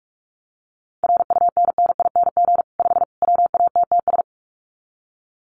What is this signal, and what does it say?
Morse code, heard as a keyed tone, about 44 wpm, 700 Hz. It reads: RUNNING 5 WATTS
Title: Common QSO Phrases 45wpm